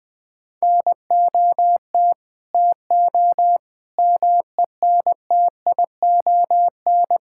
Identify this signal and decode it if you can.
Morse code, heard as a keyed tone, about 20 wpm, 700 Hz. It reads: NOT TO MENTION
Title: Sentences with 6-Word Phrases - Spoken Beginning - 20wpm